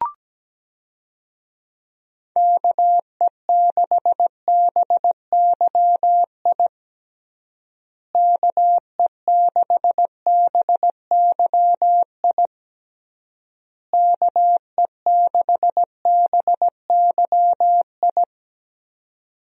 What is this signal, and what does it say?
Morse code, heard as a keyed tone, about 17 wpm, 700 Hz. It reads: KE6BYI KE6BYI KE6BYI
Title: USA Call Signs - 6 Characters - Repeated 3x - 17wpm